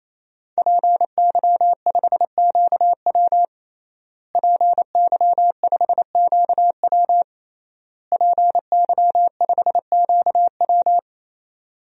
Morse code, heard as a keyed tone, about 27 wpm, 700 Hz. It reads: PY5QW PY5QW PY5QW
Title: DXCC-100 Call Signs - 5 Characters - Repeated 3x - 28wpm